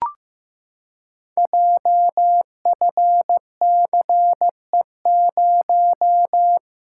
Morse code, heard as a keyed tone, about 15 wpm, 700 Hz. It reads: JFCE0
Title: Code Groups - 5 Letter:Numbers - 15wpm